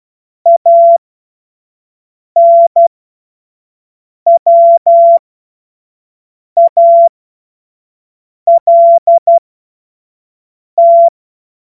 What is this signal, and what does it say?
Morse code, heard as a keyed tone, about 12 wpm, spaced as for 5 wpm, 680 Hz. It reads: ANWALT
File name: Morsekurs / Witze / Witz 07. Juni 2022 25 F70